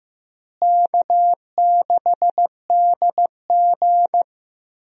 Morse code, heard as a keyed tone, about 15 wpm, 700 Hz. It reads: K6DG
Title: USA Call Signs - 4 Characters - Repeated 3x - 15wpm